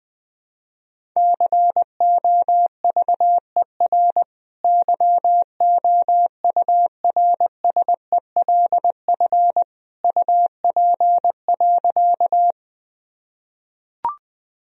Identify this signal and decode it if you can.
Morse code, heard as a keyed tone, about 20 wpm, 700 Hz. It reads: COVER YOURSELF UP.
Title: Sentences from Top 1000 Words - Review Part 1 - 20wpm